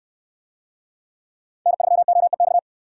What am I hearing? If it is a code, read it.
Morse code, heard as a keyed tone, about 68 wpm, 700 Hz. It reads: N3OEX